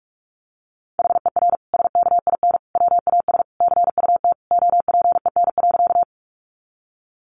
Morse code, heard as a keyed tone, about 46 wpm, 700 Hz. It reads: HER SKIN WAS CUT OPEN.
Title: Sentences from Top 1300 Words 45wpm